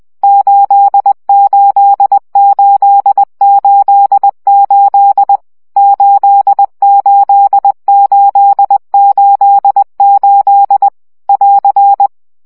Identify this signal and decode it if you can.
Morse code, heard as a keyed tone, about 20 wpm, 800 Hz. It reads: 88888 88888 <AR>
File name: CW gyakorlas 8 17 baud